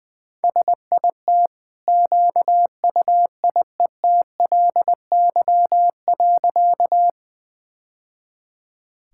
Morse code, heard as a keyed tone, about 20 wpm, 700 Hz. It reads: SIT QUIETLY.
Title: Easy Sentences with Hints - 20wpm